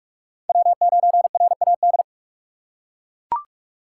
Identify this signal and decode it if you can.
Morse code, heard as a keyed tone, about 38 wpm, 700 Hz. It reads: W9RAD